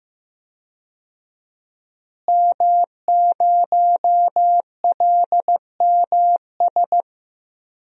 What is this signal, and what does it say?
Morse code, heard as a keyed tone, about 15 wpm, 700 Hz. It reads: M0LMS
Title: DXCC-100 Call Signs - 5 Characters - Repeated 2x - 15wpm